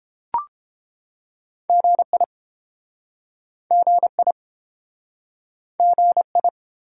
Morse code, heard as a keyed tone, about 31 wpm, 700 Hz. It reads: GI GI GI